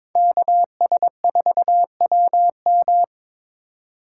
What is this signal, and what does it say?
Morse code, heard as a keyed tone, about 22 wpm, 700 Hz. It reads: KS4WM
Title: USA Call Signs - 5 Characters - 2x Character Spacing - 22wpm